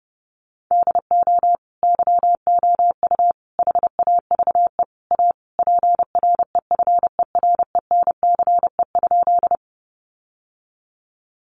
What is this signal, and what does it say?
Morse code, heard as a keyed tone, about 30 wpm, 700 Hz. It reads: DO YOU HAVE A PREFERENCE?
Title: Sentences from Top 1500 Words 30wpm